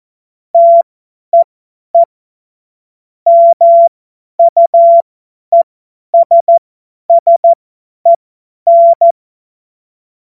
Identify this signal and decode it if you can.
Morse code, heard as a keyed tone, about 14 wpm, spaced as for 10 wpm, 680 Hz. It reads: TEE MUESSEN
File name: Morsekurs / Witze / Witz 30 Januar 2024 50 F70